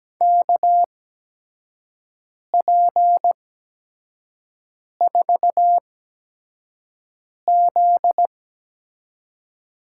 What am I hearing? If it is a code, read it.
Morse code, heard as a keyed tone, about 17 wpm, spaced as for 5 wpm, 700 Hz. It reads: KP4Z
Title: USA Call Signs - 5 Characters - 8x Character Spacing - 17wpm